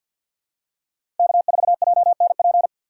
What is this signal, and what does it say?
Morse code, heard as a keyed tone, about 52 wpm, 700 Hz. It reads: K4JNP
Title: USA Call Signs - 2x Character Spacing - 50wpm